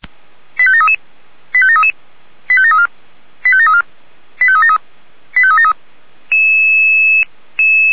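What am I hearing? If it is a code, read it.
Five-tone selective call (ZVEI), heard as quick runs of five tones, tones of about 70 ms. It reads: 85733 85733 85734 85734 85373 85373
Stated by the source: Five-tone ZVEI selective call received by SDR, fire brigade alerting (Feuerwehr Orlamünde).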